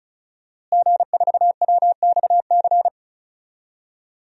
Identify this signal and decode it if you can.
Morse code, heard as a keyed tone, about 36 wpm, 700 Hz. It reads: G4WXC
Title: DXCC-100 Call Signs - 5 Characters - Repeated 2x - 35wpm